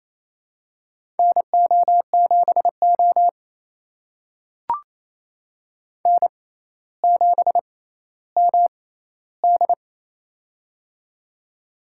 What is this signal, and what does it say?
Morse code, heard as a keyed tone, about 28 wpm, spaced as for 14 wpm, 700 Hz. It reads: NO7O N7MD
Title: USA Call Signs - 4 Characters - 6x Character Spacing - 28wpm